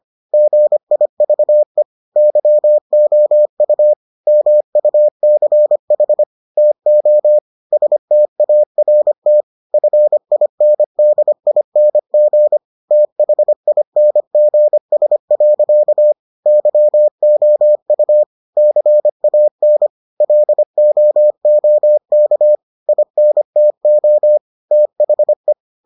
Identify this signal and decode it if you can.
Morse code, heard as a keyed tone, about 25 wpm, 600 Hz. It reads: GIVE YOU MUCH TO START FINDING THINGS. YOU CAN LOOK INTO THE